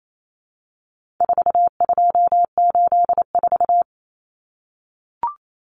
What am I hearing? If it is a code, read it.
Morse code, heard as a keyed tone, about 28 wpm, 700 Hz. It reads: 4284
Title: Common Cut Numbers - 4-Digit 28wpm